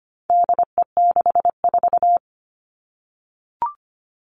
Morse code, heard as a keyed tone, about 25 wpm, 700 Hz. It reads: DE64
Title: All Cut Numbers - 4-Digit 25wpm